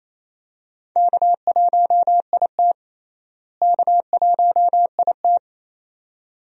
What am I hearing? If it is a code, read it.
Morse code, heard as a keyed tone, about 28 wpm, 700 Hz. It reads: K1IT K1IT
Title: USA Call Signs - 4 Characters - Repeated 2x - 28wpm